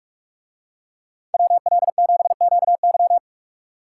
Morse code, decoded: WL7QY

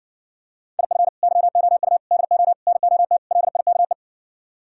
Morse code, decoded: IF YOU DO NOT LIKE

